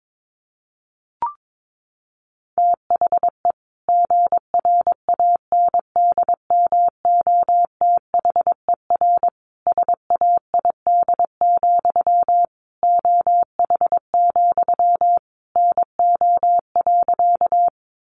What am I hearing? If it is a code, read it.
Morse code, heard as a keyed tone, about 22 wpm, 700 Hz. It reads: THE GRANDMOTHER SAID, OH, NO.
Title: Sentences from Top 1200 Words 22wpm